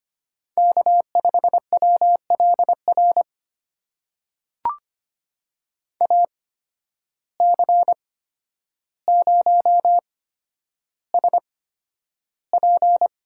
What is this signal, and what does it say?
Morse code, heard as a keyed tone, about 25 wpm, spaced as for 11 wpm, 700 Hz. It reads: K5WLR AC0SP
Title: USA Call Signs - 5 Characters - 8x Character Spacing - 25wpm